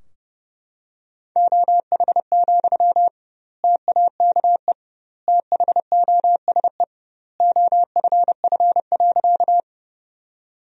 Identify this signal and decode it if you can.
Morse code, heard as a keyed tone, about 31 wpm, 700 Hz. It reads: OH, TAKE THOSE OFF.
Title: 2x Word-Spacing - Sentences from Top 500 Words - 30wpm